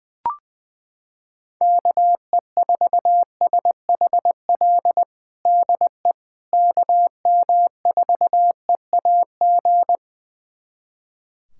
Morse code, heard as a keyed tone, about 21 wpm, 700 Hz. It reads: KE4SHL DE KM4EAG
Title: USA Callsign DE Callsign - 6 Characters - 20wpm